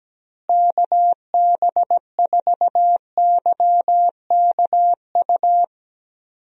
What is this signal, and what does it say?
Morse code, heard as a keyed tone, about 17 wpm, 700 Hz. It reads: KB4YKU
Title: USA Call Signs - 6 Characters - Repeated 2x - 17wpm